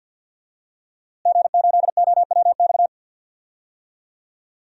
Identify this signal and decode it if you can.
Morse code, heard as a keyed tone, about 49 wpm, 700 Hz. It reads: G8OWX
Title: DXCC-100 Call Signs - 5 Characters - Repeated 3x - 50wpm